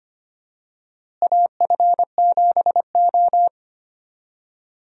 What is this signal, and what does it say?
Morse code, heard as a keyed tone, about 25 wpm, 700 Hz. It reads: AF7O
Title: USA Call Signs - 4 Characters - Repeated 2x - 25wpm